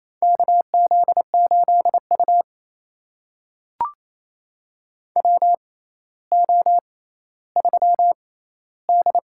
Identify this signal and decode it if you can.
Morse code, heard as a keyed tone, about 28 wpm, spaced as for 14 wpm, 700 Hz. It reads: KZ8U WO3D
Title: USA Call Signs - 4 Characters - 6x Character Spacing - 28wpm